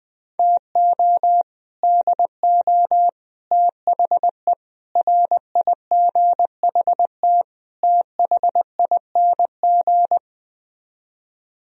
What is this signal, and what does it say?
Morse code, heard as a keyed tone, about 20 wpm, 700 Hz. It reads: TO DO THE RIGHT THING
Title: Speed-Racing - Sets of 5 Words 17wpm